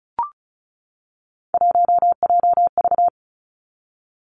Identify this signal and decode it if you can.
Morse code, heard as a keyed tone, about 34 wpm, 700 Hz. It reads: 1JV
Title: Three Challenging Letter-Number Combinations 35wpm